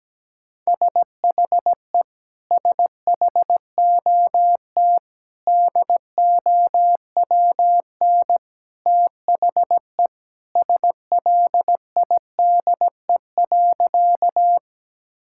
Morse code, heard as a keyed tone, about 17 wpm, 700 Hz. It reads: SHE SHOT DOWN THE SLIDE.
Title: Sentences from Top 1600 Words 17wpm